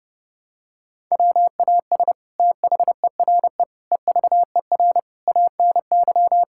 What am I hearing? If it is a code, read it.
Morse code, heard as a keyed tone, about 30 wpm, 700 Hz. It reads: WAS THERE EVER ANY
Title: Sentences from Top 1000 Words - Review Part 6 - 30wpm